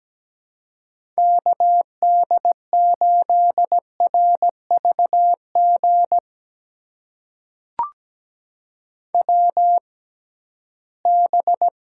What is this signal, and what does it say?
Morse code, heard as a keyed tone, about 17 wpm, spaced as for 13 wpm, 700 Hz. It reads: KD8RVG WB
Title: USA Call Signs - 6 Characters - 6x Character Spacing - 17wpm